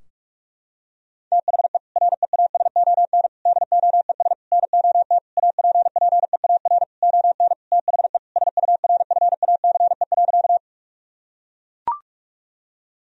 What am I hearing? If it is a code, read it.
Morse code, heard as a keyed tone, about 42 wpm, 700 Hz. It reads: THE REASON DOES NOT APPEAR ON THE SURFACE.
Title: Sentences from Top 1000 Words - Review Part 3 - 45wpm